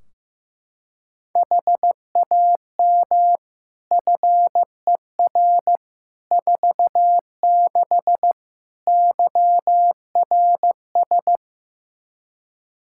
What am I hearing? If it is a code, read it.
Morse code, heard as a keyed tone, about 15 wpm, 700 Hz. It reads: HAM FER 46 YRS
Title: Practice QSO - Exchange 3 - More Info - 15wpm